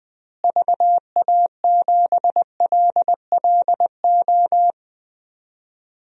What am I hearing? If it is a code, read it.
Morse code, heard as a keyed tone, about 20 wpm, 700 Hz. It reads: VA7LLO
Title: DXCC-100 Call Signs - 6 Characters - Repeated 2x - 20wpm